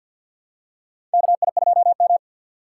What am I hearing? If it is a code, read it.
Morse code, heard as a keyed tone, about 53 wpm, 700 Hz. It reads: KI2M
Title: USA Call Signs - 2x Character Spacing - 50wpm